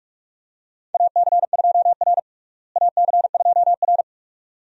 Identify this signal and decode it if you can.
Morse code, heard as a keyed tone, about 45 wpm, 700 Hz. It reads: AC2R AC2R
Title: USA Call Signs - 4 Characters - Repeated 2x - 45wpm